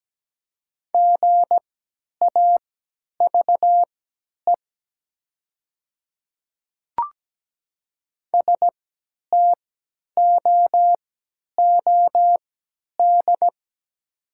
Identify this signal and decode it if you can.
Morse code, heard as a keyed tone, about 17 wpm, spaced as for 10 wpm, 700 Hz. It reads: GAVE STOOD
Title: 3x Character Spacing - Top 500 Words - 17wpm